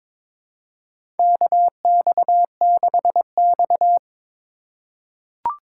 Morse code, heard as a keyed tone, about 22 wpm, 700 Hz. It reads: KX6X
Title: K1USN SST QSO Practice - 22wpm